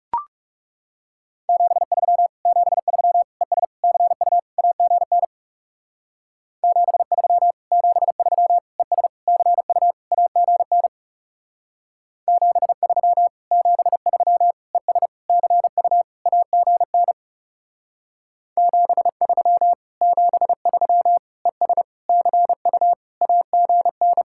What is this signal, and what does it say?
Morse code, heard as a keyed tone, about 36 wpm, 700 Hz. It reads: 73 73 ES CU AGN 73 73 ES CU AGN 73 73 ES CU AGN 73 73 ES CU AGN
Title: Speed-Racing Common QSO Phrases - 30wpm